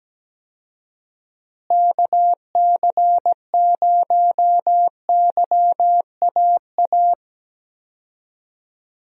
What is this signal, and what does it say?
Morse code, heard as a keyed tone, about 17 wpm, 700 Hz. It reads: KC0YAA
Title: USA Call Signs - 6 Characters - Repeated 3x - 17wpm